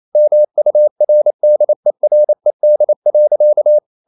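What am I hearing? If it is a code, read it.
Morse code, heard as a keyed tone, about 28 wpm, 600 Hz. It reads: MURDERED.
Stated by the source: Morse code practice at 28 wpm, text of Ken Follett's novel Never in English, from CW med Gnister.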